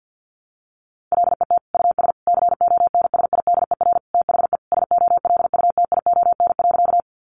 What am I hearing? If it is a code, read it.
Morse code, heard as a keyed tone, about 50 wpm, 700 Hz. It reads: LET US CONSIDER THE SOLUTION.
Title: Sentences from Top 1300 Words 50wpm